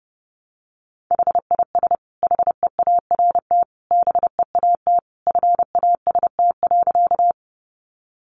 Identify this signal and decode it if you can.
Morse code, heard as a keyed tone, about 30 wpm, 700 Hz. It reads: HIS HEART BEAT FAST.
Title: Sentences from Top 1300 Words 30wpm